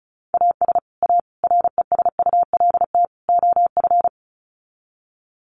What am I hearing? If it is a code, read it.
Morse code, heard as a keyed tone, about 34 wpm, 700 Hz. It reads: AS A RESULT OF
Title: Sets of 4 Words 35wpm